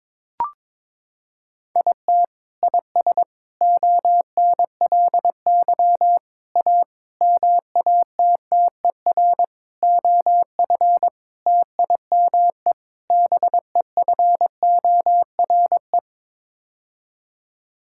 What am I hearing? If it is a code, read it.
Morse code, heard as a keyed tone, about 22 wpm, 700 Hz. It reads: IT IS ONLY A MATTER OF TIME BEFORE
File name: practice / 22wpm / Sets of 8 Words - 22wpm